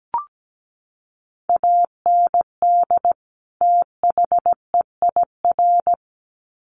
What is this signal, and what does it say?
Morse code, heard as a keyed tone, about 17 wpm, 700 Hz. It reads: AND THEIR
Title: Sets of 2 Words 17wpm